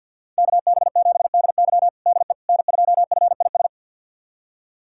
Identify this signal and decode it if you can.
Morse code, heard as a keyed tone, about 51 wpm, 700 Hz. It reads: KB7DY DE N2FIS
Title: USA Callsign DE Callsign - 5 Characters - 50wpm